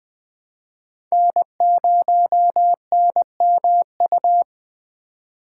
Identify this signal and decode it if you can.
Morse code, heard as a keyed tone, about 20 wpm, 700 Hz. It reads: N0NMU
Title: USA Call Signs - 3x Character Spacing - 20wpm